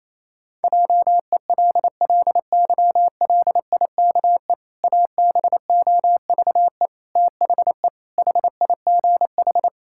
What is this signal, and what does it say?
Morse code, heard as a keyed tone, about 27 wpm, 700 Hz. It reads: JELLYLIKE ABOVE THE HIGH
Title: Pangram Challenge v2 - 28wpm